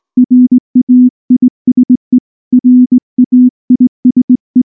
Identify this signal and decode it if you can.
Morse code, heard as a keyed tone, about 19 wpm, 250 Hz. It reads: RAISE RAISE